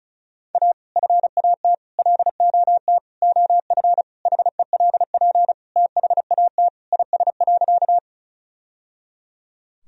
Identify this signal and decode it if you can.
Morse code, heard as a keyed tone, about 36 wpm, 700 Hz. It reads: A FAT LOT OF HELP THAT IS.
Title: Sentences from Top 1000 Words - Review Part 4 - 35wpm